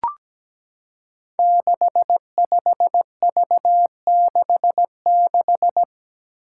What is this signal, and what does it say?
Morse code, heard as a keyed tone, about 17 wpm, 700 Hz. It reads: 65V66
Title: Five Challenging Letter-Number Combinations 17wpm